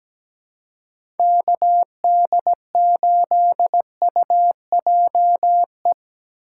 Morse code, heard as a keyed tone, about 17 wpm, 700 Hz. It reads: KD8UJE
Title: USA Call Signs - 6 Characters - 3x Character Spacing - 17wpm